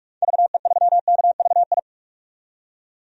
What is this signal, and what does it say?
Morse code, heard as a keyed tone, about 46 wpm, 700 Hz. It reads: VE3KVI
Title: DXCC-100 Call Signs - 6 Characters - 45wpm